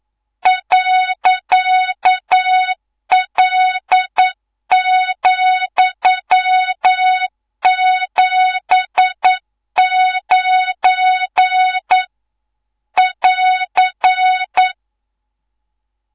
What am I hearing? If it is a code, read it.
Morse code, heard as a keyed tone, about 9 wpm, 3700 Hz, with noise. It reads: .L,79 <AR>
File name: Morze-HA2MN / hangp / 2 03 GYAKORLAT